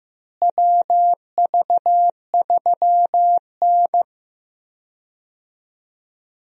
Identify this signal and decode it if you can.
Morse code, heard as a keyed tone, about 15 wpm, 700 Hz. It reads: WV3N
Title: USA Call Signs - 4 Characters - Repeated 3x - 15wpm